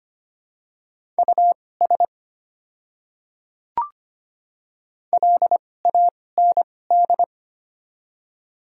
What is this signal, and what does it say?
Morse code, heard as a keyed tone, about 25 wpm, spaced as for 18 wpm, 700 Hz. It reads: US LAND
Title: 2x Character Spacing - Top 300 Words - 25wpm